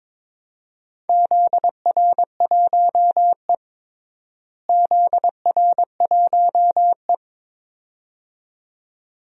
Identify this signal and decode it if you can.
Morse code, heard as a keyed tone, about 22 wpm, 700 Hz. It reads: ZR1E ZR1E
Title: DXCC-100 Call Signs - 4 Characters - Repeated 2x - 22wpm